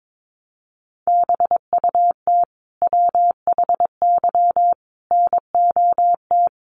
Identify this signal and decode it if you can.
Morse code, heard as a keyed tone, about 22 wpm, 700 Hz. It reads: BUT WHY NOT
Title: Sentences from Top 1400 Words 22wpm